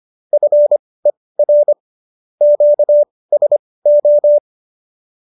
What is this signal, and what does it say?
Morse code, heard as a keyed tone, about 25 wpm, spaced as for 18 wpm, 570 Hz. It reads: FER QSO